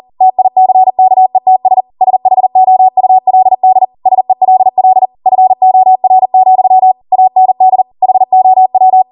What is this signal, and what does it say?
Morse code, heard as a keyed tone, about 40 wpm, 750 Hz. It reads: TICKETS SHOULD SELL FOR, AND HOW